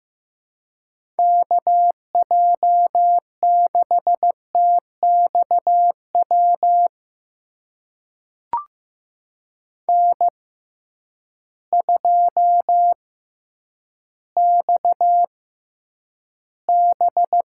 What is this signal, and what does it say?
Morse code, heard as a keyed tone, about 15 wpm, spaced as for 7 wpm, 700 Hz. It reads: KJ6TX WN2XB
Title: USA Call Signs - 6x Character Spacing - 15wpm